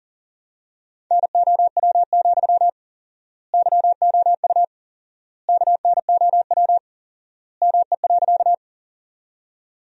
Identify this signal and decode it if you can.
Morse code, heard as a keyed tone, about 40 wpm, 700 Hz. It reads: NOW, YOU KNOW ME.
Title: 4x Word-Spacing - Sentences from Top 200 Words - 40wpm